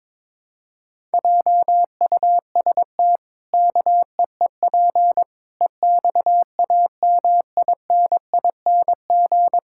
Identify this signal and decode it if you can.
Morse code, heard as a keyed tone, about 22 wpm, 700 Hz. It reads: JUST KEEP EXAMINING
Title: Pangram Challenge v2 - 22wpm